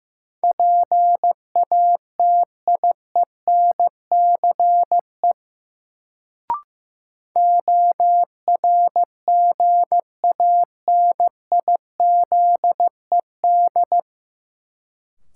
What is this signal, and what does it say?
Morse code, heard as a keyed tone, about 15 wpm, 700 Hz. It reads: PATIENCE ORGANIZED
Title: Sentences from Top 1300 Words - New Word Drill - 15wpm